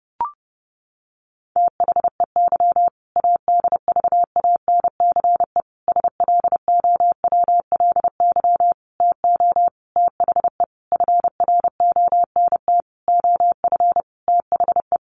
Morse code, heard as a keyed tone, about 30 wpm, 700 Hz. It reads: THEY ADVANCE SLOWLY TO THE FRONT OF THE
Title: Sentences from Top 1400 Words 30wpm